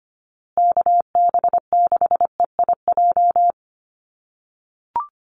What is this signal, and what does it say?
Morse code, heard as a keyed tone, about 25 wpm, 700 Hz. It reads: KB6EIJ
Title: USA Call Signs Repeated 2x 25wpm